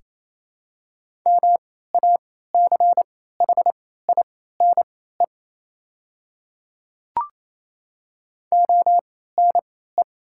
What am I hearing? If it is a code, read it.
Morse code, heard as a keyed tone, about 27 wpm, spaced as for 16 wpm, 690 Hz. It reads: MACHINE ONE
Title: 3x Character Spacing - Top 500 Words - 28wpm